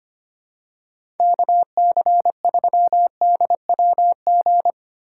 Morse code, heard as a keyed tone, about 25 wpm, 700 Hz. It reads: KC3DWG
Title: USA Call Signs - 6 Characters - 6x Character Spacing - 25wpm